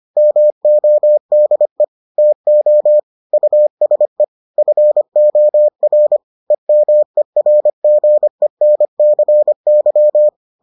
Morse code, heard as a keyed tone, about 25 wpm, 600 Hz. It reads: MODE TO USE FOR EMERGENCY